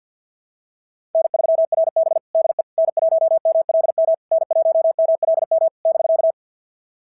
Morse code, heard as a keyed tone, about 53 wpm, 690 Hz. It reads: N3RB DE N1MLM N1MLM <BK>
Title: SOTA QSO Practice 50wpm